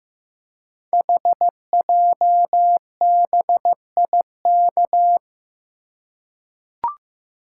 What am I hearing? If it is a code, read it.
Morse code, heard as a keyed tone, about 15 wpm, 700 Hz. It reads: HJBIK
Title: Code Groups - 5 Letters - 15wpm